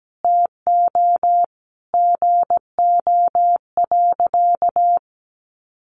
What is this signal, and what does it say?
Morse code, heard as a keyed tone, about 17 wpm, 700 Hz. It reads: TO GO.
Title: Sentences from Top 1400 Words 17wpm